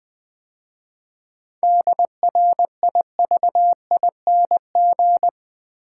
Morse code, heard as a keyed tone, about 20 wpm, 700 Hz. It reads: DRIVING
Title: 7 Letter Words - First 3 Letters Given - 20wpm